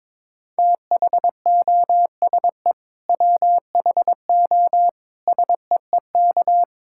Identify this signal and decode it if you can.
Morse code, heard as a keyed tone, about 22 wpm, 700 Hz. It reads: THOSE WHO SEEK
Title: Sentences from Top 1600 Words 22wpm